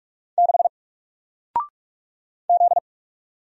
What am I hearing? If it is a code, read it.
Morse code, heard as a keyed tone, about 45 wpm, 700 Hz. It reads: 6 Z
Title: Single Letter-Number Alt 45wpm